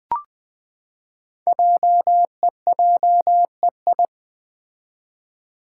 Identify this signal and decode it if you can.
Morse code, heard as a keyed tone, about 20 wpm, 700 Hz. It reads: JEJEI
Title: Code Groups - 5 Letter:Numbers - 20wpm